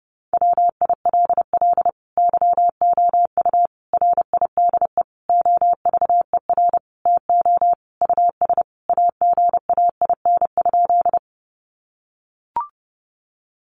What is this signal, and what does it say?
Morse code, heard as a keyed tone, about 30 wpm, 700 Hz. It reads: WILL YOU RIDE OVER TO US AGAIN?
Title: Sentences from Top 1200 Words 30wpm